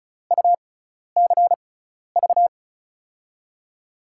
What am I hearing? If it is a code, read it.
Morse code, heard as a keyed tone, about 36 wpm, spaced as for 12 wpm, 700 Hz. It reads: UCV
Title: USA Call Signs - 6x Character Spacing - 35wpm